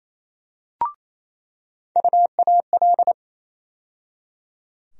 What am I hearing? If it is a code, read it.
Morse code, heard as a keyed tone, about 28 wpm, 700 Hz. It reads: UAL
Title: USA Call Sign Postfixes - 3 Letters 28wpm